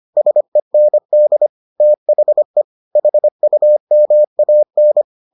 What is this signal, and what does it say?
Morse code, heard as a keyed tone, about 25 wpm, 600 Hz. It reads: SEND THE HUMAN